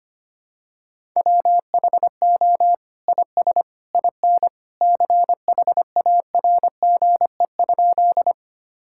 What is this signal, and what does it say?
Morse code, heard as a keyed tone, about 25 wpm, 700 Hz. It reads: WHO IS IN CHARGE?
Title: Sentences from Top 1000 Words - Review Part 6 - 25wpm